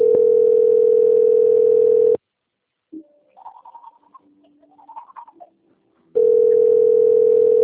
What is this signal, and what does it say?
Ringback tone (North America), 440 + 480 Hz.